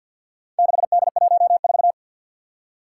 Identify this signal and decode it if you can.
Morse code, heard as a keyed tone, about 52 wpm, 700 Hz. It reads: 6D14
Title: Four Challenging Letter-Number Combinations 50wpm